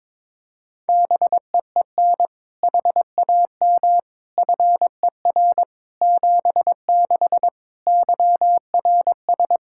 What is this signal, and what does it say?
Morse code, heard as a keyed tone, about 22 wpm, 700 Hz. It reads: BEEN HAM FER 76 YRS
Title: Practice QSO - Exchange 3 - More Info - 22wpm